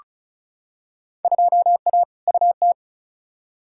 Morse code, heard as a keyed tone, about 34 wpm, 700 Hz. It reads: 2A UT
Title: Field Day Exchanges - 35wpm